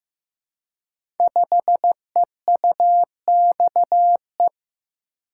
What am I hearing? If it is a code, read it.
Morse code, heard as a keyed tone, about 15 wpm, 700 Hz. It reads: 5EUXE
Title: Code Groups - 5 Letter:Numbers - 15wpm